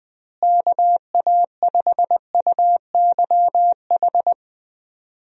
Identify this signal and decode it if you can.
Morse code, heard as a keyed tone, about 20 wpm, 700 Hz. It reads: KA5UYH
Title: USA Call Signs - 6 Characters - Repeated 3x - 20wpm